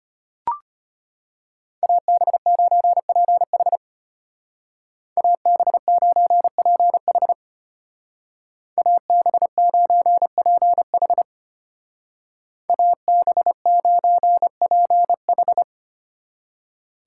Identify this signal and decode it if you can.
Morse code, heard as a keyed tone, about 33 wpm, 700 Hz. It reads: AB9PH AB9PH AB9PH AB9PH